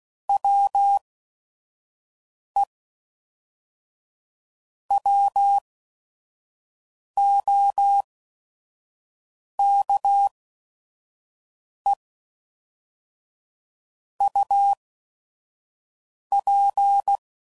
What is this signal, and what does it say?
Morse code, heard as a keyed tone, about 16 wpm, spaced as for 5 wpm, 800 Hz. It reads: WE WOKE UP
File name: CW Code Groups @5WPM / Weather Sets / Weather 3 17